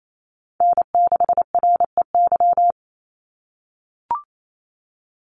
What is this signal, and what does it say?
Morse code, heard as a keyed tone, about 28 wpm, 700 Hz. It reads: N6REY
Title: USA Call Signs 28wpm